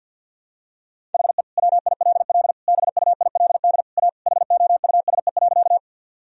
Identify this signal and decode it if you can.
Morse code, heard as a keyed tone, about 52 wpm, 700 Hz. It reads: HE WILL BUILD A HOUSE.
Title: Sentences from Top 1000 Words - Review Part 1 - 50wpm